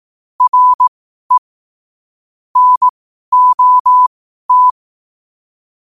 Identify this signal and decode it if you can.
Morse code, heard as a keyed tone, about 18 wpm, spaced as for 12 wpm, 1000 Hz. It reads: RE NOT
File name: qotdfiles / 12wpm